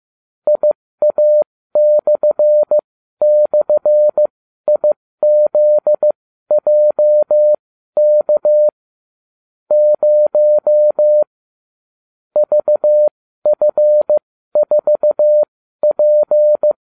Morse code, heard as a keyed tone, about 15 wpm, spaced as for 12 wpm, 600 Hz. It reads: IA//IZJK 0 VF4P